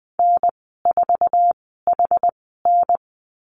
Morse code, heard as a keyed tone, about 20 wpm, spaced as for 15 wpm, 700 Hz. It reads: N4HN
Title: 069 Lesson 12g - Letter d - Call Signs 20wpm